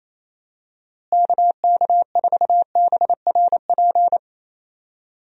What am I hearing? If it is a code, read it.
Morse code, heard as a keyed tone, about 28 wpm, 700 Hz. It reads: KK4BRP
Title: USA Call Signs - 6 Characters - Repeated 2x - 28wpm